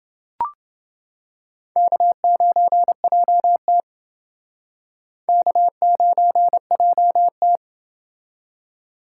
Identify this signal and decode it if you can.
Morse code, heard as a keyed tone, about 29 wpm, 700 Hz. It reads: K9JT K9JT